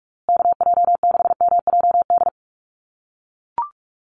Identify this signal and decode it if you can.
Morse code, heard as a keyed tone, about 44 wpm, 690 Hz. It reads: KJ6MJD
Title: USA Call Signs Repeated 3x 45wpm